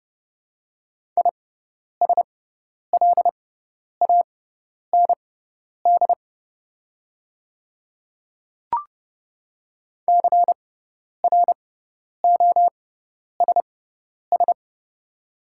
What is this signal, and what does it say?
Morse code, heard as a keyed tone, about 30 wpm, spaced as for 10 wpm, 700 Hz. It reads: ISLAND CROSS
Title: 6x Character Spacing - Top 500 Words - 30wpm